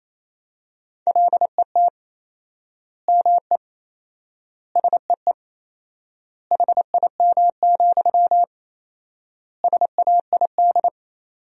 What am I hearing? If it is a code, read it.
Morse code, heard as a keyed tone, about 28 wpm, 700 Hz. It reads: LET ME SEE HIM, SAID
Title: 4x Word-Spacing - Sentences from Top 400 Words - 28wpm